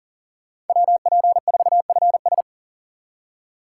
Morse code, heard as a keyed tone, about 50 wpm, 700 Hz. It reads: WP4FS